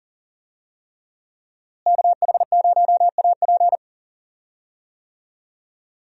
Morse code, heard as a keyed tone, about 40 wpm, 700 Hz. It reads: KH0AP
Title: USA Call Signs - 5 Characters - Repeated 3x - 40wpm